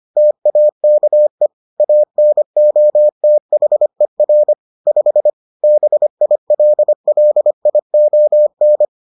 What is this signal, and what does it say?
Morse code, heard as a keyed tone, about 25 wpm, 600 Hz. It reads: TAKE ANOTHER 5 BILLION